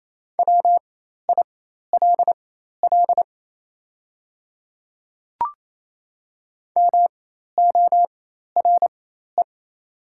Morse code, heard as a keyed tone, about 28 wpm, spaced as for 13 wpm, 700 Hz. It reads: WILL MORE
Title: 4x Character Spacing - Top 100 Words - 28wpm